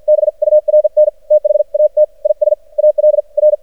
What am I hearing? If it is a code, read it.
Morse code, heard as a keyed tone, about 53 wpm, spaced as for 45 wpm, 610 Hz. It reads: BURN THAT IS AL R
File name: archived / CRSnet